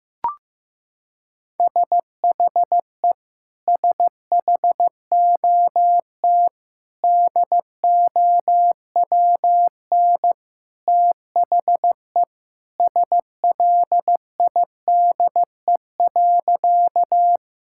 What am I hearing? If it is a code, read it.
Morse code, heard as a keyed tone, about 15 wpm, 700 Hz. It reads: SHE SHOT DOWN THE SLIDE.
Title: Sentences from Top 1600 Words 15wpm